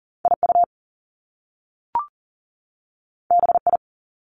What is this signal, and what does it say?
Morse code, heard as a keyed tone, about 40 wpm, 700 Hz. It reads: IU BI
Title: Top Two Letter Combinations in Words 40wpm